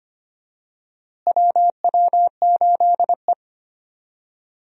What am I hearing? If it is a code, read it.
Morse code, heard as a keyed tone, about 25 wpm, 700 Hz. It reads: WW8E
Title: USA Call Signs - 4 Characters - Repeated 3x - 25wpm